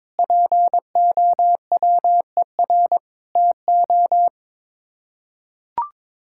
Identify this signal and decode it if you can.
Morse code, heard as a keyed tone, about 22 wpm, 700 Hz. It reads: POWER TO
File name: practice / 22wpm / Sets of 8 Words - 22wpm